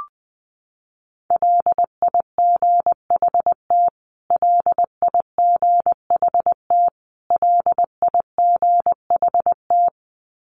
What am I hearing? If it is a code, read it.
Morse code, heard as a keyed tone, about 19 wpm, 700 Hz. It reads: LIGHT LIGHT LIGHT
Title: Top 500 Words Repeated 20wpm